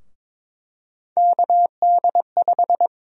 Morse code, heard as a keyed tone, about 22 wpm, 700 Hz. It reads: KD5
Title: USA Call Sign Prefixes - Repeated 2x - 22wpm